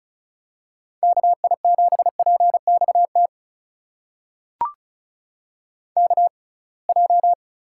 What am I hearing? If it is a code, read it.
Morse code, heard as a keyed tone, about 36 wpm, spaced as for 27 wpm, 700 Hz. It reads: KI7PXT KJ
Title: USA Call Signs - 6 Characters - 6x Character Spacing - 35wpm